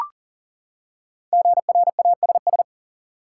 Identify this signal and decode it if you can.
Morse code, heard as a keyed tone, about 40 wpm, 700 Hz. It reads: GRASS
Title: Five Letter Words - 40wpm